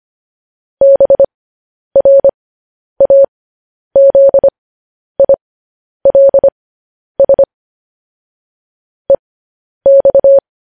Morse code, heard as a keyed tone, about 25 wpm, spaced as for 10 wpm, 550 Hz. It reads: BRAZILS EX
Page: news headlines in morse code at 10 wpm